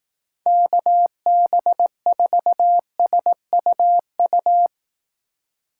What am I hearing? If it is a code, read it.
Morse code, heard as a keyed tone, about 18 wpm, 700 Hz. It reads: KB4SUU
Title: Speed Racing - USA 2x3 Call Signs 15wpm